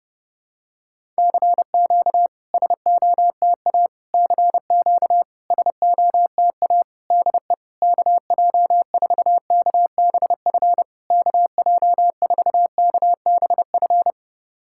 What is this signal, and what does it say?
Morse code, heard as a keyed tone, about 30 wpm, 700 Hz. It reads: CQ SOTA CQ SOTA DE KJ4KBF KJ4KBF
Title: SOTA QSO Practice 30wpm